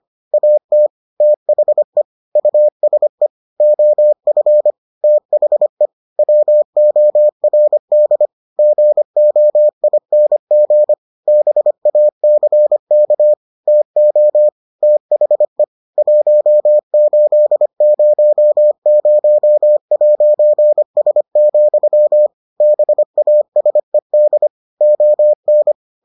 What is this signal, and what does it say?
Morse code, heard as a keyed tone, about 25 wpm, 600 Hz. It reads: AT THE USE OF THE WORD GOING BACK TO THE 1800'S, BASED ON